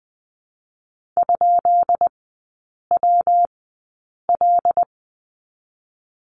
Morse code, heard as a keyed tone, about 20 wpm, spaced as for 8 wpm, 700 Hz. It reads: ?WL
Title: 152 Lesson 25e - Letter p - Three Characters 20wpm